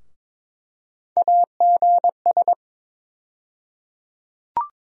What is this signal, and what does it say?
Morse code, heard as a keyed tone, about 23 wpm, 700 Hz. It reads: AGS
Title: USA Call Sign Postfixes - 3 Letters 22wpm